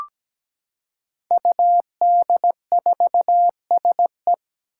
Morse code, heard as a keyed tone, about 17 wpm, 700 Hz. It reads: UD4SE
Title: Five Challenging Letter-Number Combinations 17wpm